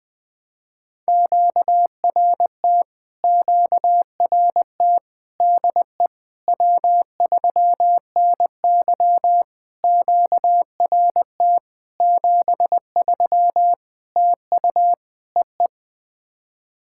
Morse code, heard as a keyed tone, about 21 wpm, 700 Hz. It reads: QRT QRT DE W3NY QRT 73 TU EE
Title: POTA QSO Practice - 20wpm